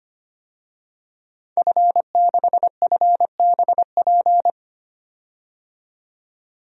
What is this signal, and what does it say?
Morse code, heard as a keyed tone, about 25 wpm, 700 Hz. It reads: F6FBP
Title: DXCC-100 Call Signs - 5 Characters - 25wpm